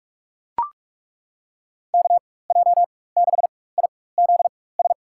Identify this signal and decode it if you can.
Morse code, heard as a keyed tone, about 42 wpm, spaced as for 20 wpm, 700 Hz. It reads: KJ6IZS
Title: USA Call Signs - 6 Characters - 4x Character Spacing - 45wpm